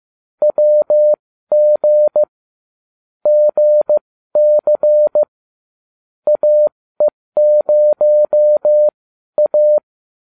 Morse code, heard as a keyed tone, about 15 wpm, spaced as for 12 wpm, 600 Hz. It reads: WG GC AE0A